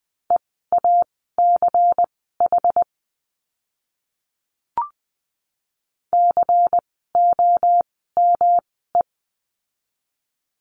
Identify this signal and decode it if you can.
Morse code, heard as a keyed tone, about 20 wpm, spaced as for 14 wpm, 700 Hz. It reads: EACH COME
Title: 192 Lesson 31f - Letter v - Words 20wpm